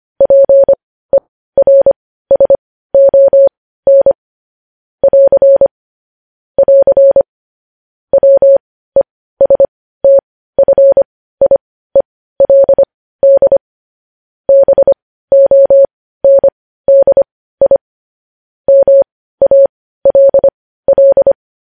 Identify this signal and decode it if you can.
Morse code, heard as a keyed tone, about 25 wpm, spaced as for 15 wpm, 550 Hz. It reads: PERSON <AR> <AR> WESTFIELD BONDI MALL